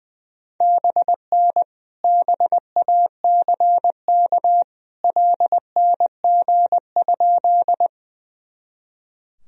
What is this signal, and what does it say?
Morse code, heard as a keyed tone, about 20 wpm, 700 Hz. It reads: BN BACK LNG?
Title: Sentences from Top 200 Words and Abbreviations 20wpm